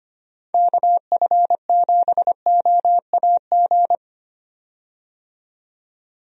Morse code, heard as a keyed tone, about 25 wpm, 700 Hz. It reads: KF7OAG
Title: USA Call Signs - 6 Characters - 25wpm